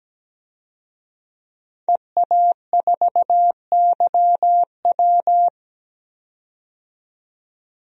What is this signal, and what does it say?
Morse code, heard as a keyed tone, about 17 wpm, 700 Hz. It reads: EA4YW
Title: DXCC-100 Call Signs - 5 Characters - 17wpm